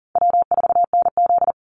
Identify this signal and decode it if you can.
Morse code, heard as a keyed tone, about 37 wpm, 700 Hz. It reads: W4NZ
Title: Easy Call Signs Repeated 3x - 40wpm